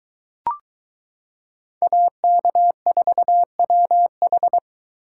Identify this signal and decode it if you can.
Morse code, heard as a keyed tone, about 22 wpm, 700 Hz. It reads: AK4WH